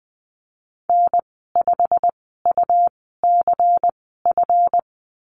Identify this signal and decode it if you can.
Morse code, heard as a keyed tone, about 20 wpm, spaced as for 14 wpm, 700 Hz. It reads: N5UCF
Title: 186 Lesson 30g - Letter b - Call Signs 20wpm